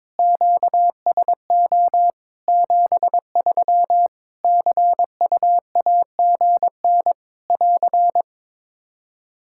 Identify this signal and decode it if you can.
Morse code, heard as a keyed tone, about 22 wpm, 700 Hz. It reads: QSO 73 CUAGN <AR>
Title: Practice QSO - Exchange 4 - Ending - 22wpm